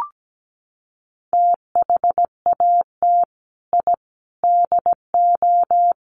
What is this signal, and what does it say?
Morse code, heard as a keyed tone, about 17 wpm, 700 Hz. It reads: THAT I DO
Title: Sets of 5 Words Encore2 17wpm